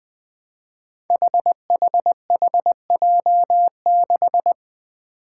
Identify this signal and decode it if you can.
Morse code, heard as a keyed tone, about 20 wpm, 700 Hz. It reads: HHHJ6
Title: Five Challenging Letter-Number Combinations 20wpm